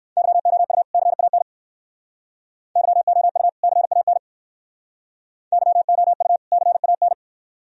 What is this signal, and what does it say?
Morse code, heard as a keyed tone, about 59 wpm, 690 Hz. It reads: YOU CAN YOU CAN YOU CAN